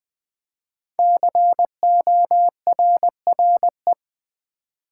Morse code, heard as a keyed tone, about 20 wpm, 700 Hz. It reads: CORRE
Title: Build A Word - 5 Letters At A Time - 10-Letter Words - 20wpm